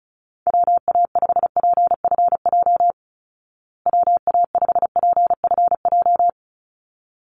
Morse code, heard as a keyed tone, about 34 wpm, 700 Hz. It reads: WA5PFJ WA5PFJ
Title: Call Signs Repeated 3x - 35wpm